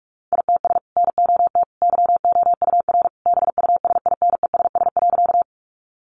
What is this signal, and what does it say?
Morse code, heard as a keyed tone, about 44 wpm, 700 Hz. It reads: ITS NOT YOUR BUSINESS.
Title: Sentences from Top 600 Words 45wpm